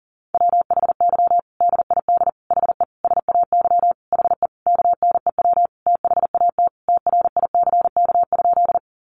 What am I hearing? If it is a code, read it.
Morse code, heard as a keyed tone, about 40 wpm, 700 Hz. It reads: WHY DID HE SAY HE KNEW THAT TRICK?
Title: Sentences from Top 1500 Words 40wpm